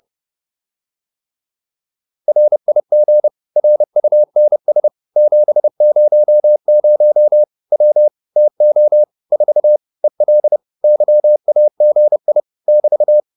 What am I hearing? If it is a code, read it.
Morse code, heard as a keyed tone, about 30 wpm, 600 Hz. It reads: RIG RUNS 700 W TO 4 EL YAGI <BT>
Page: advanced-practice-files